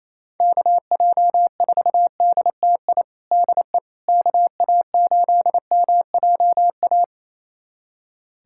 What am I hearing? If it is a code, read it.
Morse code, heard as a keyed tone, about 28 wpm, 700 Hz. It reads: KJ4DTI DE KA8MJA
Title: USA Callsign DE Callsign 28wpm